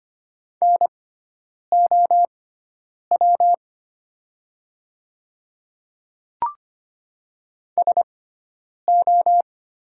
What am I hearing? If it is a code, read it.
Morse code, heard as a keyed tone, about 25 wpm, spaced as for 9 wpm, 700 Hz. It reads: NOW SO
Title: 6x Character Spacing - Top 100 Words - 25wpm